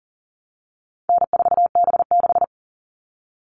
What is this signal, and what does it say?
Morse code, heard as a keyed tone, about 40 wpm, 700 Hz. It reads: N4B6
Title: All Cut Numbers - 4-Digit 40wpm